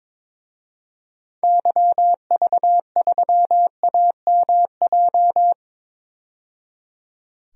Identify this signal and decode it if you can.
Morse code, heard as a keyed tone, about 22 wpm, 700 Hz. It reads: YV3AMJ
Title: DXCC-100 Call Signs - 6 Characters - 22wpm